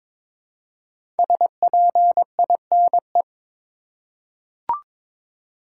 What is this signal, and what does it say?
Morse code, heard as a keyed tone, about 23 wpm, 700 Hz. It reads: SPINE
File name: 5 Letter Words - First 3 Letters Given - 22wpm